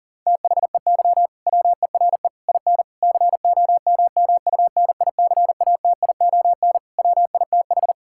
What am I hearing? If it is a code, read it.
Morse code, heard as a keyed tone, about 40 wpm, 700 Hz. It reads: THEY WERE IN COMMUNICATION WITH
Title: Sentences from Top 1000 Words - Review Part 6 - 40wpm